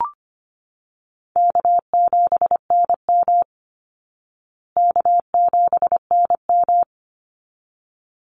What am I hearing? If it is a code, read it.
Morse code, heard as a keyed tone, about 25 wpm, 700 Hz. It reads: K7NM K7NM
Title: Call Signs Repeated 3x - 25wpm